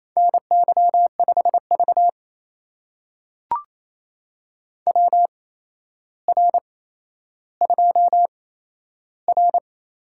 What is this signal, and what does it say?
Morse code, heard as a keyed tone, about 28 wpm, spaced as for 12 wpm, 700 Hz. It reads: NY5V WR2R
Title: USA Call Signs - 4 Characters - 8x Character Spacing - 28wpm